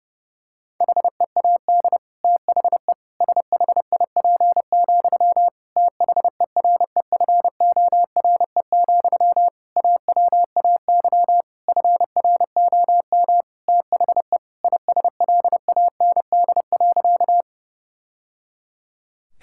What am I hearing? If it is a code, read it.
Morse code, heard as a keyed tone, about 32 wpm, 700 Hz. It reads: HEAD THE SHIP, THEREFORE, AWAY FROM THE ISLAND.
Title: Sentences from Top 1000 Words - Review Part 5 - 30wpm